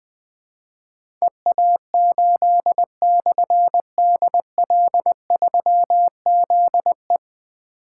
Morse code, heard as a keyed tone, about 20 wpm, 700 Hz. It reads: EA8/DL3ZE
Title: Difficult Call Signs 20wpm